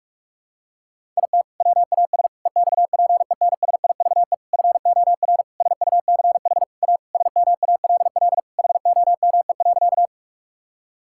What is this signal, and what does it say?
Morse code, heard as a keyed tone, about 46 wpm, 700 Hz. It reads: IT WAS EXPENSIVE FOR SUCH A SMALL HOME.
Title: Sentences from Top 1000 Words - Review Part 6 - 45wpm